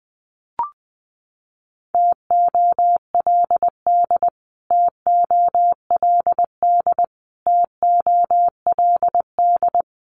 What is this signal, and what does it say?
Morse code, heard as a keyed tone, about 20 wpm, 700 Hz. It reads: TOLD TOLD TOLD
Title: Top 500 Words Repeated 20wpm